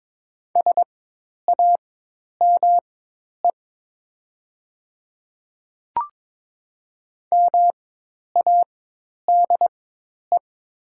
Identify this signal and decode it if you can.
Morse code, heard as a keyed tone, about 22 wpm, spaced as for 10 wpm, 700 Hz. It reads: SAME MADE
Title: 4x Character Spacing - Top 200 Words - 22wpm